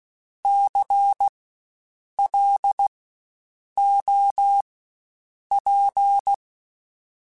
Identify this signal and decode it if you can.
Morse code, heard as a keyed tone, about 16 wpm, spaced as for 7 wpm, 800 Hz. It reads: CLOP
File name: Spaced Alphas / Spaced Alphas mp3s / Cs sp @15